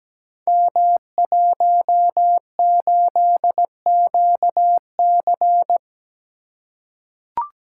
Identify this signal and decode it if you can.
Morse code, heard as a keyed tone, about 17 wpm, 700 Hz. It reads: M18QC
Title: Code Groups - 5 Letter:Numbers - 17wpm